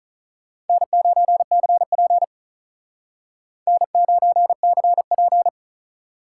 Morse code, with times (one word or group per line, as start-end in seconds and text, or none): 0.7-2.2 N9CP
3.7-5.5 N9CP